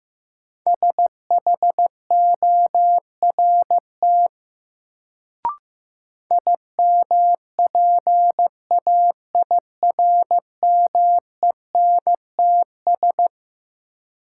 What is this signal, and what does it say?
Morse code, heard as a keyed tone, about 15 wpm, 700 Hz. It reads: SHORT IMPAIRMENTS
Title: Sets of 5 Words Encore - New Word Drill - 15wpm